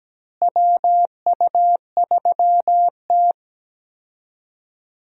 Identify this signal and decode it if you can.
Morse code, heard as a keyed tone, about 17 wpm, 700 Hz. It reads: WU3T
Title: USA Call Signs - 4 Characters - Repeated 2x - 17wpm